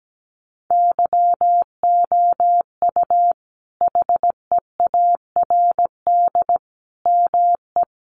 Morse code, heard as a keyed tone, about 17 wpm, 700 Hz. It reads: YOU HEARD ME
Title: Sentences from Top 500 Words 17wpm